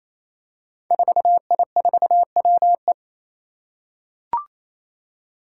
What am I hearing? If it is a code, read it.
Morse code, heard as a keyed tone, about 28 wpm, 700 Hz. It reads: 4I4WE
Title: Five Challenging Letter-Number Combinations 28wpm